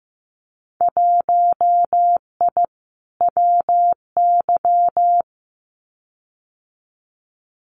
Morse code, heard as a keyed tone, about 15 wpm, 700 Hz. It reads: 1I WY
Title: Winter Field Day Exchanges Encore 15wpm